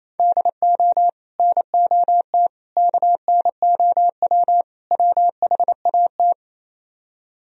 Morse code, heard as a keyed tone, about 38 wpm, 700 Hz. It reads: DO NOT KNOW WHAT